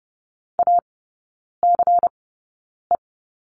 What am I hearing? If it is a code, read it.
Morse code, heard as a keyed tone, about 30 wpm, spaced as for 9 wpm, 690 Hz. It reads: ACE
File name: Lesson 16e - Letter c - Three Characters 30wpm